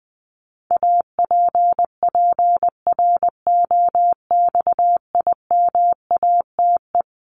Morse code, heard as a keyed tone, about 20 wpm, 700 Hz. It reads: APPROXIMATE
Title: LY Suffix - Top 100 Words with Base Words 20wpm